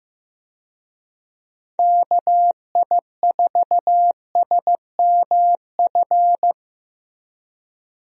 Morse code, heard as a keyed tone, about 15 wpm, 700 Hz. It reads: KI4SMF
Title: USA Call Signs - 6 Characters - Repeated 3x - 15wpm